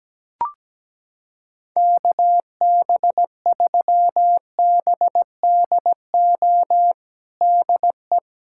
Morse code, heard as a keyed tone, about 17 wpm, 700 Hz. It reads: KB3BDO DE
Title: USA Callsign DE Callsign 17wpm